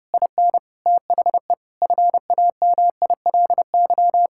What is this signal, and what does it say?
Morse code, heard as a keyed tone, about 30 wpm, 700 Hz. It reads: IN THE FAMILY
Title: Sentences from Top 1000 Words - Review Part 6 - 30wpm